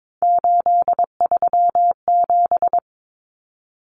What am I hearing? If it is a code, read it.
Morse code, heard as a keyed tone, about 22 wpm, 700 Hz. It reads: 837
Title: Numbers - 3-Digit 22wpm